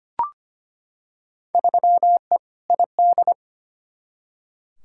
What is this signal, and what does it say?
Morse code, heard as a keyed tone, about 25 wpm, 700 Hz. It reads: 3E ID
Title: Field Day Exchanges Encore - 25wpm